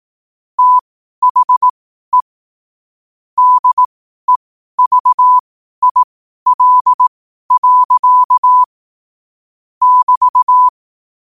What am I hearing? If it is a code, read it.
Morse code, heard as a keyed tone, about 18 wpm, spaced as for 12 wpm, 1000 Hz. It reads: THE DEVIL. <BT>
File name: qotdfiles / 12wpm